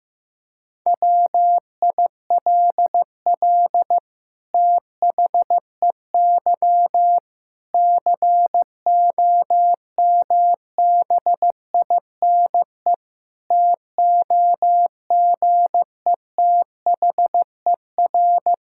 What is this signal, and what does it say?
Morse code, heard as a keyed tone, about 16 wpm, 700 Hz. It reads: WILL THEY COMBINE TOGETHER
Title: Sentences from Top 1600 Words 15wpm